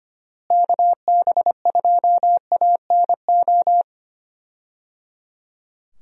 Morse code, heard as a keyed tone, about 25 wpm, 700 Hz. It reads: KB2ANO
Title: USA Call Signs - 6 Characters - Repeated 3x - 25wpm